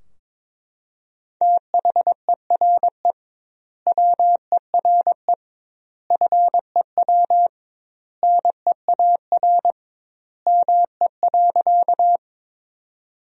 Morse code, heard as a keyed tone, about 22 wpm, 700 Hz. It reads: THERE WERE FEW NEAR ME.
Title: 2x Word-Spacing - Sentences from Top 300 Words - 22wpm